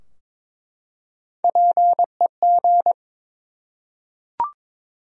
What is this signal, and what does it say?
Morse code, heard as a keyed tone, about 22 wpm, 700 Hz. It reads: PEG
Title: USA Common Names - Max 3 Letters - 22wpm